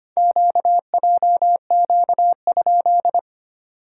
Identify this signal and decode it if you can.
Morse code, heard as a keyed tone, about 25 wpm, 690 Hz. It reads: QJQ?
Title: SOTA QSO Practice 25wpm